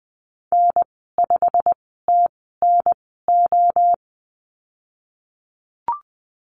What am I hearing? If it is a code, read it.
Morse code, heard as a keyed tone, about 20 wpm, spaced as for 15 wpm, 700 Hz. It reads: N5TNO
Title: 173 Lesson 28g - Number 9 - Call Signs 20wpm